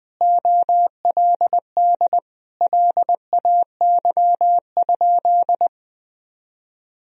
Morse code, heard as a keyed tone, about 20 wpm, 700 Hz. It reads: OLD LAY?
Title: Sentences from Top 400 Words and Abbreviations 20wpm